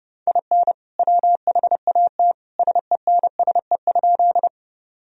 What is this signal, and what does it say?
Morse code, heard as a keyed tone, about 30 wpm, 700 Hz. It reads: IN WHAT SENSE?
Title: Sentences from Top 1000 Words - Review Part 4 - 30wpm